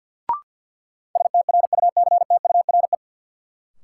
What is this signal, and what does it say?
Morse code, heard as a keyed tone, about 52 wpm, 700 Hz. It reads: STRUCTURE
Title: Sentences from Top 1000 Words - Review Part 6 - New Word Drill - 50wpm